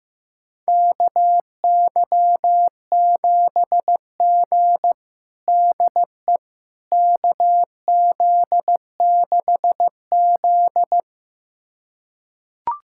Morse code, heard as a keyed tone, about 15 wpm, 700 Hz. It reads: KY7G DE KZ6Z
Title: USA Callsign DE Callsign - 4 Characters - 15wpm